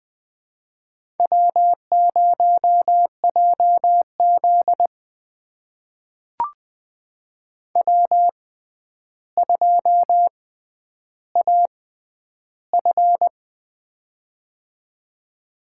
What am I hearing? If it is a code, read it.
Morse code, heard as a keyed tone, about 20 wpm, spaced as for 10 wpm, 700 Hz. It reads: W0JZ W2AF
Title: USA Call Signs - 4 Characters - 6x Character Spacing - 20wpm